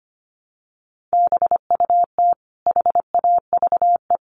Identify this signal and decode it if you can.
Morse code, heard as a keyed tone, about 25 wpm, 700 Hz. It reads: BUT HAVE
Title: Sentences from Top 1500 Words 25wpm